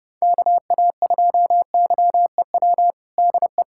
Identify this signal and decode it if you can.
Morse code, heard as a keyed tone, about 32 wpm, 700 Hz. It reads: KA2YEW DE
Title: USA Callsign DE Callsign - 6 Characters - 30wpm